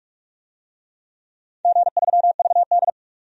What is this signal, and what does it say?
Morse code, heard as a keyed tone, about 46 wpm, 700 Hz. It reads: G3VD
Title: DXCC-100 Call Signs - 4 Characters - 45wpm